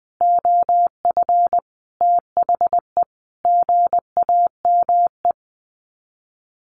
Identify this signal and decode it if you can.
Morse code, heard as a keyed tone, about 20 wpm, 700 Hz. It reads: OF THE GAME
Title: Sets of 5 Words Encore2 20wpm